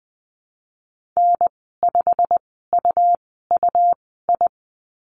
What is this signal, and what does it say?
Morse code, heard as a keyed tone, about 20 wpm, spaced as for 14 wpm, 700 Hz. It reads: N5UUI
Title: 122 Lesson 20g - Number 3 - Call Signs 20wpm